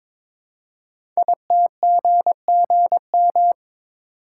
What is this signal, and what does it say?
Morse code, heard as a keyed tone, about 22 wpm, 700 Hz. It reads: ITGGM
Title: Code Groups - 5 Letters - 22wpm